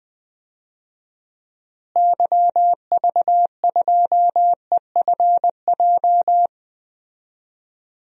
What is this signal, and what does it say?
Morse code, heard as a keyed tone, about 20 wpm, 700 Hz. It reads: YV2EFJ
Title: DXCC-100 Call Signs - 6 Characters - Repeated 2x - 20wpm